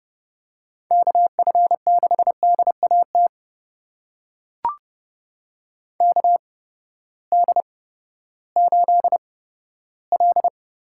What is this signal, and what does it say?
Morse code, heard as a keyed tone, about 30 wpm, spaced as for 15 wpm, 700 Hz. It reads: KF6DAT KD8L
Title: USA Call Signs - 6 Characters - 8x Character Spacing - 30wpm